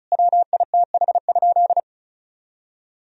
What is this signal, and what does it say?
Morse code, heard as a keyed tone, about 36 wpm, 700 Hz. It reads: WITH?
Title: Sentences from Top 1000 Words - Review Part 4 - 35wpm